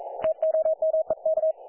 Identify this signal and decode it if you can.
Morse code, heard as a keyed tone, about 42 wpm, 630 Hz. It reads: TOMEK
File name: dlazaawansowanych / nagraniazpasma / websdr recording 3564.7kHz